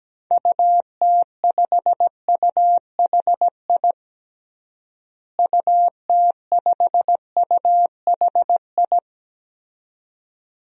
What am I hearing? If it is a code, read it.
Morse code, heard as a keyed tone, about 17 wpm, 700 Hz. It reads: UT5UHI UT5UHI
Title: DXCC-100 Call Signs - 6 Characters - Repeated 2x - 17wpm